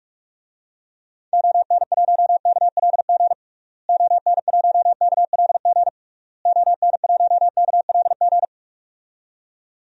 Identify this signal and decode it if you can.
Morse code, heard as a keyed tone, about 43 wpm, 700 Hz. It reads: ON1KLG ON1KLG ON1KLG
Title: DXCC-100 Call Signs - 6 Characters - Repeated 3x - 45wpm